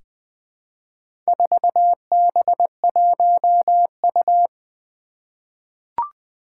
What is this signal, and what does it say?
Morse code, heard as a keyed tone, about 20 wpm, 700 Hz. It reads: 4B1U
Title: Four Challenging Letter-Number Combinations 20wpm